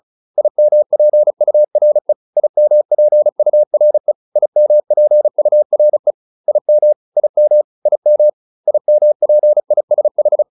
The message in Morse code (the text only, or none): IMPURE IMPURE IMPURE IM IM IM IMPISH